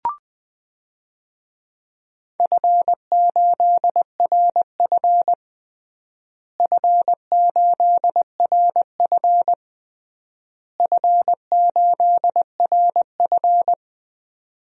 Morse code, heard as a keyed tone, about 20 wpm, 700 Hz. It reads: F8RF F8RF F8RF
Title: DXCC-100 Call Signs - 4 Characters - Repeated 3x - 20wpm